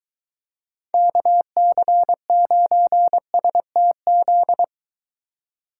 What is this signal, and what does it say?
Morse code, heard as a keyed tone, about 23 wpm, 700 Hz. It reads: KC9STZ